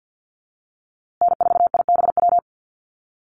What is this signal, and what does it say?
Morse code, heard as a keyed tone, about 50 wpm, 700 Hz. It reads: N4IBW
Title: USA Call Signs Repeated 3x 50wpm